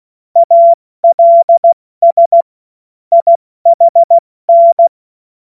Morse code, heard as a keyed tone, about 16 wpm, 680 Hz. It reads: ALS IHN
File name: Morsekurs / Witze / Witz 15. November 2022 70 F80